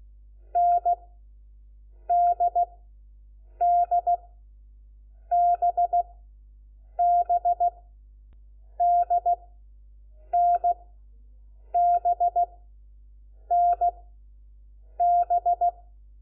Morse code, heard as a keyed tone, about 16 wpm, spaced as for 7 wpm, 700 Hz. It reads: NDDBB DNBNB
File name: Lecciones / CLASE Nº4